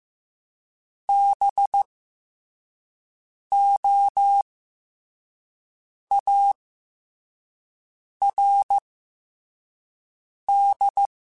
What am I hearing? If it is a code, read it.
Morse code, heard as a keyed tone, about 15 wpm, 800 Hz. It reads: B O A R D
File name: Special INGs / Special INGs-mp3s / SlowToFaster ing @13